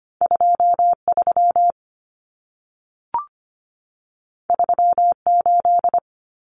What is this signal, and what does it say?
Morse code, heard as a keyed tone, about 25 wpm, 700 Hz. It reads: 23 38
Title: Numbers - 2-Digit 25wpm